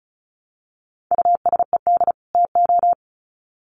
Morse code, heard as a keyed tone, about 34 wpm, 700 Hz. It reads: USED TO
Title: Sets of 2 Words 35wpm